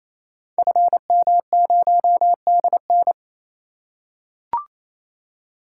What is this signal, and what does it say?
Morse code, heard as a keyed tone, about 29 wpm, 700 Hz. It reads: FM0DN
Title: Code Groups - 5 Letter:Numbers - 28wpm